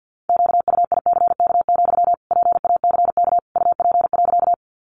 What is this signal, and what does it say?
Morse code, heard as a keyed tone, about 50 wpm, 700 Hz. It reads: QUICK, PACK UP.
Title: Sentences from Top 1300 Words 50wpm